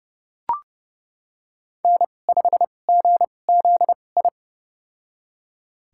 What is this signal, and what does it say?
Morse code, heard as a keyed tone, about 32 wpm, spaced as for 22 wpm, 700 Hz. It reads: N5GZI
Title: USA Call Signs - 2x Character Spacing - 30wpm